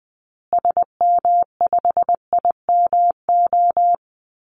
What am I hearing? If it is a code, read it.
Morse code, heard as a keyed tone, about 20 wpm, 700 Hz. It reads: SM5IMO
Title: Call Signs Repeated 3x - 20wpm